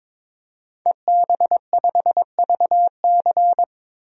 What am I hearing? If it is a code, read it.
Morse code, heard as a keyed tone, about 22 wpm, 700 Hz. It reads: EB5VC
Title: DXCC-100 Call Signs - 5 Characters - 22wpm